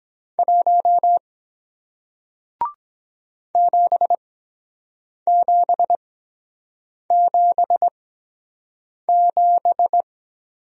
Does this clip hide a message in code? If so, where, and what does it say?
Yes: Morse code from 0.4 to 10.0 s, heard as a keyed tone, about 23 wpm, spaced as for 7 wpm, 700 Hz. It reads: 1 7777